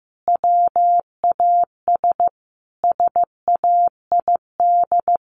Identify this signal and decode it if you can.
Morse code, heard as a keyed tone, about 15 wpm, 700 Hz. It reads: WAS SAID
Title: Sentences from Top 1100 Words 15wpm